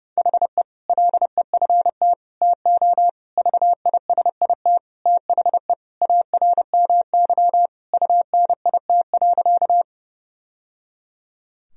Morse code, heard as a keyed tone, about 30 wpm, 700 Hz. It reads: HE LEFT TO VISIT THE ARMY UNIT.
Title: Sentences from Top 1000 Words - Review Part 2 - 30wpm